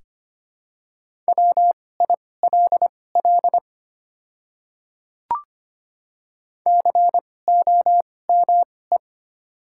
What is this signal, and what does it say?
Morse code, heard as a keyed tone, about 25 wpm, spaced as for 18 wpm, 700 Hz. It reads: WILL COME
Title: 2x Character Spacing - Top 100 Words - 25wpm